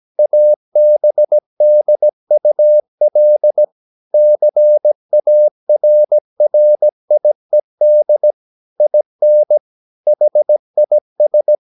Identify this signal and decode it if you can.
Morse code, heard as a keyed tone, about 17 wpm, 600 Hz. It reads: ABDUL CARRIED IN HIS
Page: Romanen Never af Ken Follet på engelsk.